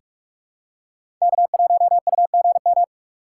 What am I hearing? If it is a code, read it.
Morse code, heard as a keyed tone, about 46 wpm, 700 Hz. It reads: K1UGM
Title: USA Call Signs - 5 Characters - 3x Character Spacing - 45wpm